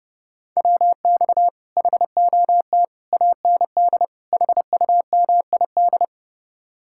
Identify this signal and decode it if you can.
Morse code, heard as a keyed tone, about 38 wpm, 700 Hz. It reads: WX HOT AND HUMID